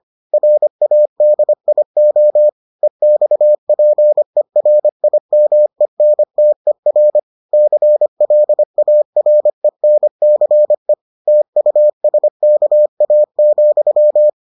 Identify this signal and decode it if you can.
Morse code, heard as a keyed tone, about 25 wpm, 600 Hz. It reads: RADIO EXPERIMENTER CLARENCE TUSKA,